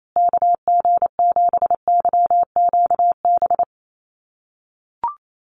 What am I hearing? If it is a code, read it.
Morse code, heard as a keyed tone, about 28 wpm, 700 Hz. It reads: KG7YQB
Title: USA Call Signs Repeated 2x 28wpm